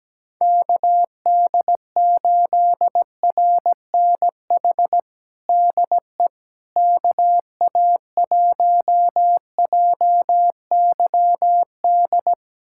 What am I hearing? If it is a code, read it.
Morse code, heard as a keyed tone, about 17 wpm, 700 Hz. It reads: KD8RNH DE KA1JYD
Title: USA Callsign DE Callsign 17wpm